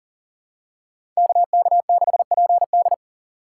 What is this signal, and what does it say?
Morse code, heard as a keyed tone, about 40 wpm, 690 Hz. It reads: KK6PD
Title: USA Call Signs - 2x Character Spacing - 40wpm